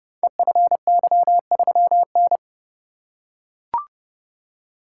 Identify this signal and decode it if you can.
Morse code, heard as a keyed tone, about 30 wpm, 700 Hz. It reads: EFY3N
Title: Code Groups - 5 Letter:Numbers - 30wpm